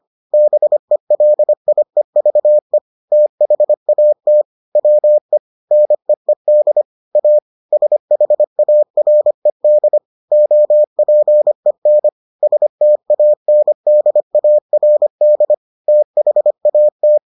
Morse code, heard as a keyed tone, about 25 wpm, 600 Hz. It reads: BELIEVE THAT WE NEED A SHARED OPEN STANDARD THAT